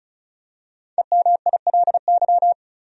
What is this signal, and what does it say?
Morse code, heard as a keyed tone, about 34 wpm, 700 Hz. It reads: EMILY
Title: USA Common Names - 35wpm